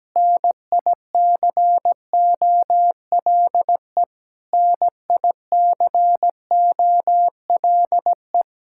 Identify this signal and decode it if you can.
Morse code, heard as a keyed tone, about 17 wpm, 700 Hz. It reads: NICOLE NICOLE
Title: Practice QSO - Exchange 1 - Beginning - 17wpm